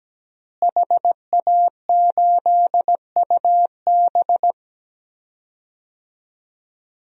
Morse code, heard as a keyed tone, about 17 wpm, 700 Hz. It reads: HA8UB
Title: DXCC-100 Call Signs - 5 Characters - Repeated 3x - 17wpm